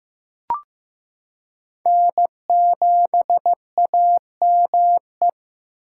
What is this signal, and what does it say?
Morse code, heard as a keyed tone, about 15 wpm, 700 Hz. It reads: N7AME
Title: K1USN SST QSO Practice - 15wpm